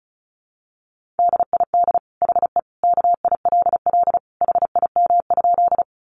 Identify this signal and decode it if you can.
Morse code, heard as a keyed tone, about 34 wpm, 700 Hz. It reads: DID HE KILL HIM?
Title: Sentences from Top 1400 Words 35wpm